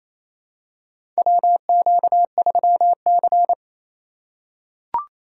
Morse code, heard as a keyed tone, about 28 wpm, 700 Hz. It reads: WQ3C
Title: USA Call Signs - 2x Character Spacing - 28wpm